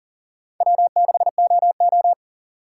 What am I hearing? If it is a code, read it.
Morse code, heard as a keyed tone, about 39 wpm, 700 Hz. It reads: W6OO
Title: USA Call Signs - 3x Character Spacing - 40wpm